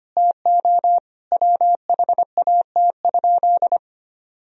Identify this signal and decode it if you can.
Morse code, heard as a keyed tone, about 25 wpm, 700 Hz. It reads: TO WHAT?
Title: Sentences from Top 1000 Words - Review Part 4 - 25wpm